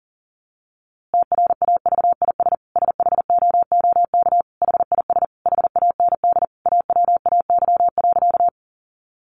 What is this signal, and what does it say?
Morse code, heard as a keyed tone, about 37 wpm, 700 Hz. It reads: TRAVIS SHOOK HIS HAND AWAY.
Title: A Sound Of Thunder 40wpm